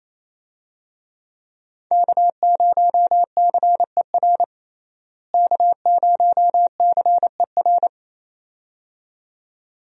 Morse code, heard as a keyed tone, about 28 wpm, 700 Hz. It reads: K0CER K0CER
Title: USA Call Signs - 5 Characters - Repeated 2x - 28wpm